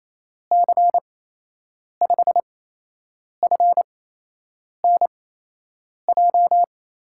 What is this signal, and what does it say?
Morse code, heard as a keyed tone, about 28 wpm, spaced as for 8 wpm, 700 Hz. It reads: C5FNJ
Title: USA Call Signs - 6 Characters - 8x Character Spacing - 28wpm